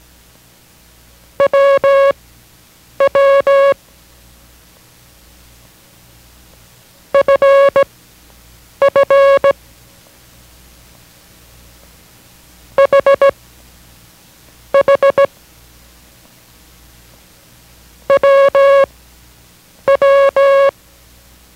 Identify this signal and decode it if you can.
Morse code, heard as a keyed tone, about 17 wpm, spaced as for 7 wpm, 550 Hz. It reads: WW FF HH WW